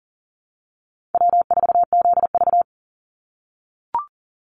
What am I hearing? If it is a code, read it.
Morse code, heard as a keyed tone, about 40 wpm, 700 Hz. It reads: W4ZV
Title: NA SOTA Chaser Callsigns 40wpm